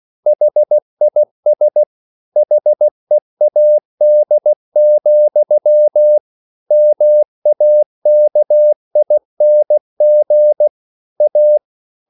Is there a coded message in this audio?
Yes: Morse code from 0.3 to 11.6 s, heard as a keyed tone, about 16 wpm, 600 Hz. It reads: HIS HEAD, MAKING A